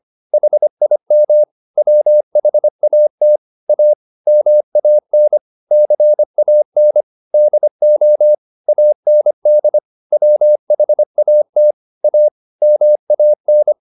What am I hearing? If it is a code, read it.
Morse code, heard as a keyed tone, about 25 wpm, 600 Hz. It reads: HIM WHAT A MAN CAN DO AND WHAT A MAN
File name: the old man and the sea part 1 25wpm